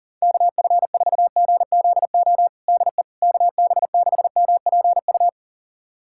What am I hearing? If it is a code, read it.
Morse code, heard as a keyed tone, about 40 wpm, 700 Hz. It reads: KF4GZO DE KB6MPU
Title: USA Callsign DE Callsign - 6 Characters - 40wpm